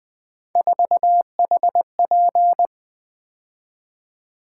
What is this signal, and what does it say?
Morse code, heard as a keyed tone, about 20 wpm, 700 Hz. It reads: 4HP
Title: Three Letter-Number Combinations - 20wpm